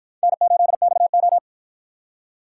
Morse code, heard as a keyed tone, about 53 wpm, 700 Hz. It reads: N8XO